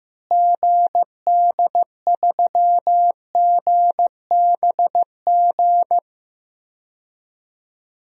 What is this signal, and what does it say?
Morse code, heard as a keyed tone, about 16 wpm, 700 Hz. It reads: GD3GBG
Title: DXCC-100 Call Signs - 6 Characters - Repeated 3x - 15wpm